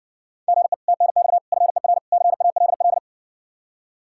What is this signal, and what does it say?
Morse code, heard as a keyed tone, about 60 wpm, 700 Hz. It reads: GE TNX FER CALL